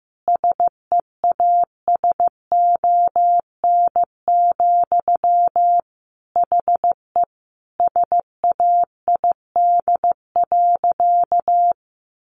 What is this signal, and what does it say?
Morse code, heard as a keyed tone, about 15 wpm, 700 Hz. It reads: SEASON, HE SAID.
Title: Sentences from Top 1400 Words 15wpm